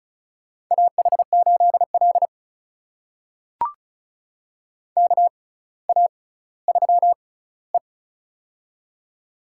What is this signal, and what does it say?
Morse code, heard as a keyed tone, about 36 wpm, spaced as for 18 wpm, 700 Hz. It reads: AH8L KA3E
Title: USA Call Signs - 4 Characters - 6x Character Spacing - 35wpm